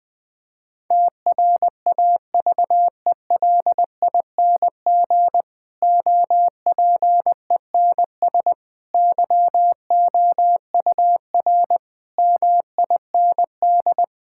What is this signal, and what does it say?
Morse code, heard as a keyed tone, about 20 wpm, 700 Hz. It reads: TRAVELING OPENS YOUR MIND
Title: Difficult Sentences with Hints - 20wpm